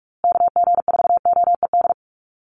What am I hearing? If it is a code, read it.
Morse code, heard as a keyed tone, about 44 wpm, 700 Hz. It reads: KG4OED
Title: USA Call Signs Repeated 3x 45wpm